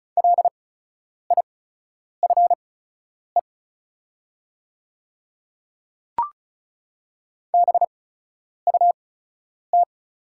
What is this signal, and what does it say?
Morse code, heard as a keyed tone, about 36 wpm, spaced as for 10 wpm, 690 Hz. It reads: LIFE BUT
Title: 8x Character Spacing - Top 300 Words - 35wpm